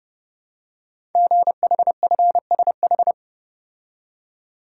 Morse code, heard as a keyed tone, about 28 wpm, 700 Hz. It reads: GHFSH
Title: Code Groups - 5 Letters - 30wpm